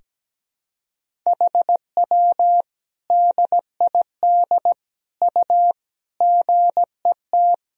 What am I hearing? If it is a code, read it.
Morse code, heard as a keyed tone, about 17 wpm, 700 Hz. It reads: HW DID U GET
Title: Sentences from Top 200 Words and Abbreviations 17wpm